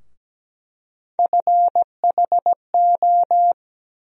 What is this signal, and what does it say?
Morse code, heard as a keyed tone, about 17 wpm, 700 Hz. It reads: FHO
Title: USA Call Sign Postfixes - 3 Letters 17wpm